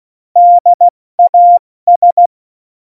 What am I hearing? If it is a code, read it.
Morse code, heard as a keyed tone, about 16 wpm, 700 Hz. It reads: DAS
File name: Morsekurs / Witze / Witz 01. Mai 2018, 70 F80